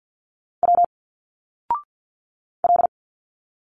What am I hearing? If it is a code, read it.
Morse code, heard as a keyed tone, about 49 wpm, 700 Hz. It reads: W L
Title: Single Letters 50wpm